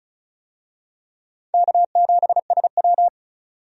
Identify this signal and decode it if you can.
Morse code, heard as a keyed tone, about 36 wpm, 700 Hz. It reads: K7SW
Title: USA Call Signs - 4 Characters - 35wpm